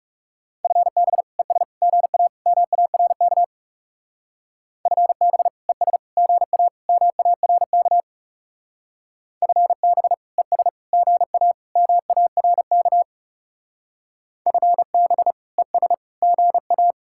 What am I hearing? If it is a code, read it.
Morse code, heard as a keyed tone, about 39 wpm, 700 Hz. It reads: FB ES GA MARK FB ES GA MARK FB ES GA MARK FB ES GA